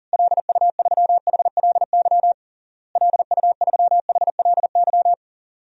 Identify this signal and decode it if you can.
Morse code, heard as a keyed tone, about 40 wpm, 700 Hz. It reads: LU3HLY LU3HLY
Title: DXCC-100 Call Signs - 6 Characters - Repeated 3x - 40wpm